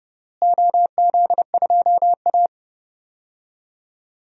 Morse code, heard as a keyed tone, about 32 wpm, 700 Hz. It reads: OZ2A
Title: DXCC-100 Call Signs - 4 Characters - Repeated 3x - 30wpm